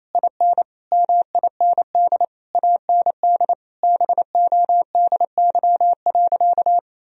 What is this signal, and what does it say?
Morse code, heard as a keyed tone, about 28 wpm, 700 Hz. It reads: IN MIND AND BODY.
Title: Sentences from Top 1000 Words - Review Part 2 - 28wpm